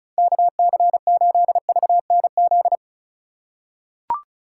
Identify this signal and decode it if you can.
Morse code, heard as a keyed tone, about 34 wpm, 690 Hz. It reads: KC8VNZ
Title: USA Call Signs - 3x Character Spacing - 35wpm